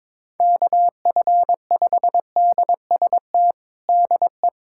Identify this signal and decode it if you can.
Morse code, heard as a keyed tone, about 22 wpm, 700 Hz. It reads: KF5DST DE
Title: USA Callsign DE Callsign 22wpm